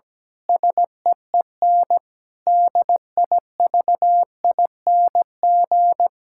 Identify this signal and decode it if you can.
Morse code, heard as a keyed tone, about 17 wpm, 700 Hz. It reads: SEEN DIVING
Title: Pangram Challenge v2 - 17wpm